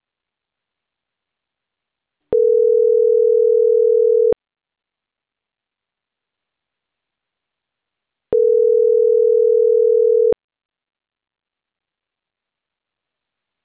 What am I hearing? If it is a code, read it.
Ringback tone (North America), 440 + 480 Hz.